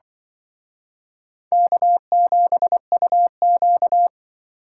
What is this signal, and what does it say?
Morse code, heard as a keyed tone, about 24 wpm, 700 Hz. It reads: K7UQ
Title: Speed Racing - USA Call Signs 20wpm